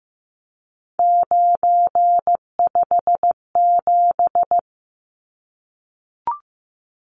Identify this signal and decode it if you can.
Morse code, heard as a keyed tone, about 14 wpm, 700 Hz. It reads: 957
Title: Numbers - 3-Digit 15wpm